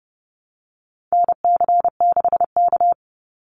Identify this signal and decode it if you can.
Morse code, heard as a keyed tone, about 29 wpm, 700 Hz. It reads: NC6K
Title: Call Signs - 30wpm